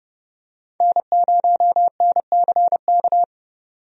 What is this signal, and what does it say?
Morse code, heard as a keyed tone, about 30 wpm, 700 Hz. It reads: N0NCK
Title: USA Call Signs - 5 Characters - 30wpm